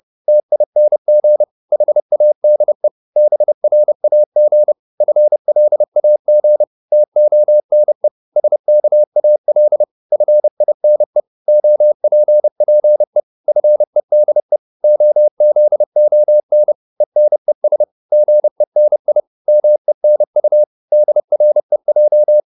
Morse code, heard as a keyed tone, about 30 wpm, 600 Hz. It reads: TING HADE BRAG FLAG TONE SKAL FINE OPPE FEDE OZON ENES GENI MENU DREJ
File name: Korte ord DK 30wpm